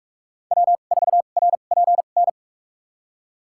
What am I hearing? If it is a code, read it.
Morse code, heard as a keyed tone, about 41 wpm, spaced as for 31 wpm, 700 Hz. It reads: W4RPN
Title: USA Call Signs - 2x Character Spacing - 45wpm